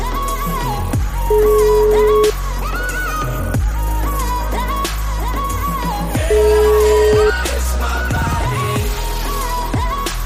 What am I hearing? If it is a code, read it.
Ringback tone (Europe, 425 Hz), with noise.